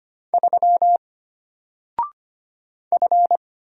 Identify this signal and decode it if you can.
Morse code, heard as a keyed tone, about 25 wpm, 700 Hz. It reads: 3 F
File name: Single-Letter-Number-25wpm